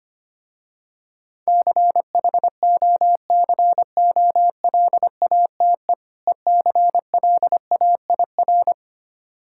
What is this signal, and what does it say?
Morse code, heard as a keyed tone, about 25 wpm, 700 Hz. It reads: CHOCOLATE ECLAIR
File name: practice / 25wpm / Difficult Words with Hints - 25wpm